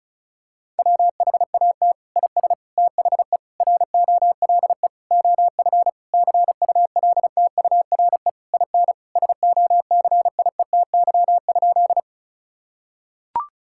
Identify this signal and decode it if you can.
Morse code, heard as a keyed tone, about 34 wpm, 700 Hz. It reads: WHAT IS THE ROLE OF CULTURE IN SOCIETY?
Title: Sentences from Top 1000 Words - Review Part 6 - 35wpm